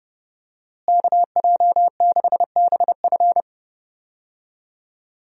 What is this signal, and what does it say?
Morse code, heard as a keyed tone, about 30 wpm, 700 Hz. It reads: KJ6BF
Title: USA Call Signs - 5 Characters - Repeated 2x - 30wpm